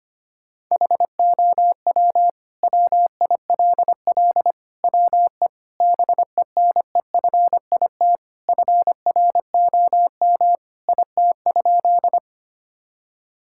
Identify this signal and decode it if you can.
Morse code, heard as a keyed tone, about 25 wpm, 700 Hz. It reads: HOW WILL WE BENEFIT FROM IT?
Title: Sentences from Top 1000 Words - Review Part 6 - 25wpm